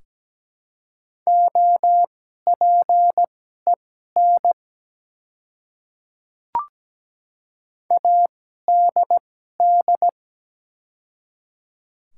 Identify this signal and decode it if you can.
Morse code, heard as a keyed tone, about 17 wpm, spaced as for 12 wpm, 700 Hz. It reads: OPEN ADD
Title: 2x Character Spacing - Top 300 Words - 17wpm